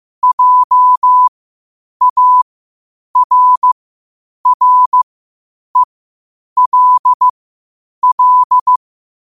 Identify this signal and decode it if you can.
Morse code, heard as a keyed tone, about 15 wpm, spaced as for 8 wpm, 1000 Hz. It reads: JARRELL
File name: qotdfiles / 8wpm